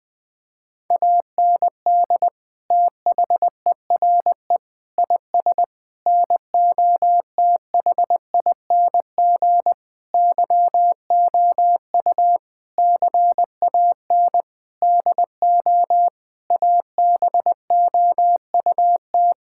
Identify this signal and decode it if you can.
Morse code, heard as a keyed tone, about 20 wpm, 700 Hz. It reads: AND THERE IS NOTHING YOU CAN DO ABOUT
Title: Sentences with 8-Word Phrases - 20wpm